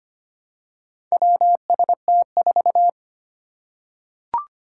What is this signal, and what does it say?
Morse code, heard as a keyed tone, about 25 wpm, 700 Hz. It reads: WST4
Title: Four Challenging Letter-Number Combinations 25wpm